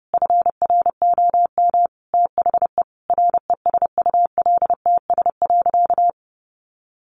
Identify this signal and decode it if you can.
Morse code, heard as a keyed tone, about 30 wpm, 700 Hz. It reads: FROM THE RESULTS.
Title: A Sound Of Thunder 30wpm